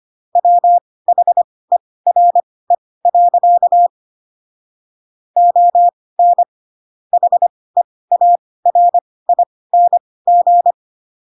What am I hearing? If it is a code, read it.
Morse code, heard as a keyed tone, about 25 wpm, spaced as for 18 wpm, 700 Hz. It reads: WHERE. ON HEARING
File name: Pinocchio / Pinocchio ch2 12 25 ik1hge